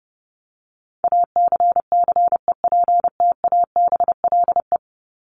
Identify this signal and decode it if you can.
Morse code, heard as a keyed tone, about 30 wpm, 700 Hz. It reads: ACCEPTABLE
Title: Top 1501-1750 Words 30wpm